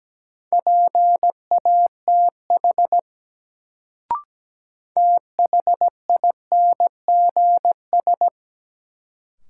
Morse code, heard as a keyed tone, about 17 wpm, 700 Hz. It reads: PATH THINGS
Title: Sentences from Top 1400 Words - New Word Drill Alt - 17wpm